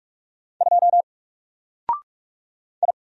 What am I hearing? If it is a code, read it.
Morse code, heard as a keyed tone, about 45 wpm, 700 Hz. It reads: 2 I
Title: Single Letter-Number 45wpm